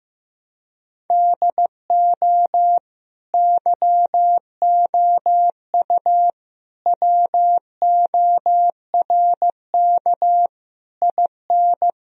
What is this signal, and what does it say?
Morse code, heard as a keyed tone, about 15 wpm, 700 Hz. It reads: DO YOU WORK IN
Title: Sentences from Top 1000 Words - Review Part 4 - 15wpm